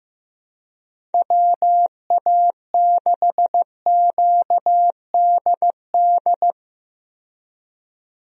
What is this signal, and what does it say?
Morse code, heard as a keyed tone, about 15 wpm, 700 Hz. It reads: WA6QDD
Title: USA Call Signs - 6 Characters - 15wpm